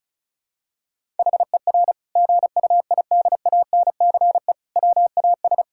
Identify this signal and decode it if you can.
Morse code, heard as a keyed tone, about 36 wpm, 700 Hz. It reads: HER GUIDANCE WAS
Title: Sentences from Top 1600 Words 35wpm